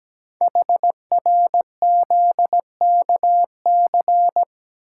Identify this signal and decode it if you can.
Morse code, heard as a keyed tone, about 17 wpm, 700 Hz. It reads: HRZKC
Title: Code Groups - 5 Letters - 17wpm